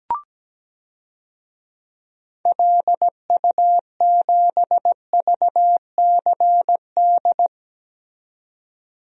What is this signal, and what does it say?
Morse code, heard as a keyed tone, about 17 wpm, 700 Hz. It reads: LU7VCD
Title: DXCC-100 Call Signs - 6 Characters - 17wpm